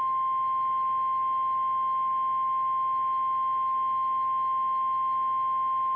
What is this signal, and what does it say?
Emergency Alert System (SAME) tones (the 1050 Hz attention tone). It is heard from 0.8 to 5.7 s.